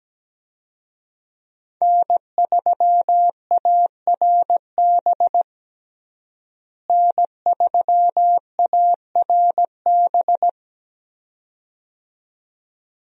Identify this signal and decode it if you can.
Morse code, heard as a keyed tone, about 17 wpm, 700 Hz. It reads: N3ARB N3ARB
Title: USA Call Signs - 5 Characters - Repeated 2x - 17wpm